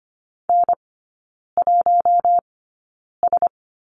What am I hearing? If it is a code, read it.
Morse code, heard as a keyed tone, about 25 wpm, spaced as for 9 wpm, 690 Hz. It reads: N1S
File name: Lesson 14e - Number 2 - Three Characters 25wpm